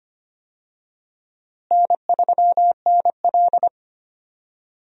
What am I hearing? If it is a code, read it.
Morse code, heard as a keyed tone, about 25 wpm, 700 Hz. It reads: N3NL
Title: USA Call Signs - 4 Characters - Repeated 2x - 25wpm